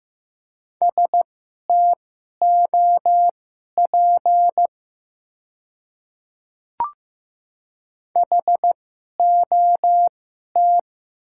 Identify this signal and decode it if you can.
Morse code, heard as a keyed tone, about 15 wpm, spaced as for 11 wpm, 700 Hz. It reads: STOP HOT
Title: 2x Character Spacing - Top 500 Words - 15wpm